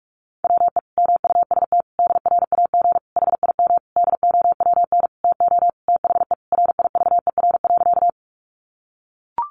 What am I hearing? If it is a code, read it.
Morse code, heard as a keyed tone, about 44 wpm, 700 Hz. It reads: WE MUST DRAG HIM DOWN TO THE RIVER.
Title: Sentences from Top 1500 Words 45wpm